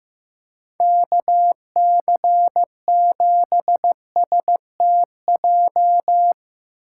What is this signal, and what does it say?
Morse code, heard as a keyed tone, about 15 wpm, 700 Hz. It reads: KC7STJ
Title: USA Call Signs - 6x Character Spacing - 15wpm